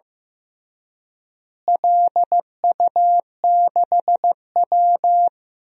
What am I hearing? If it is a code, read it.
Morse code, heard as a keyed tone, about 15 wpm, 700 Hz. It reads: LU6W
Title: DXCC-100 Call Signs - 4 Characters - Repeated 3x - 15wpm